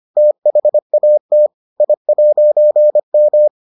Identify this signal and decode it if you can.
Morse code, heard as a keyed tone, about 25 wpm, 600 Hz. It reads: THAT I'M